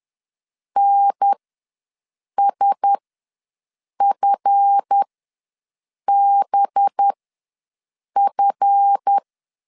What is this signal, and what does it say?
Morse code, heard as a keyed tone, about 10 wpm, spaced as for 6 wpm, 800 Hz. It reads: NSFBF
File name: LEZIONI / 9 2 ws 10wpm 4130